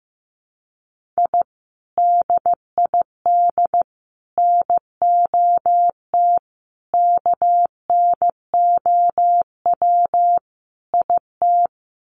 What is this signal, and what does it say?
Morse code, heard as a keyed tone, about 15 wpm, 700 Hz. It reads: I DID NOT KNOW IT
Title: Sets of 5 Words Encore2 15wpm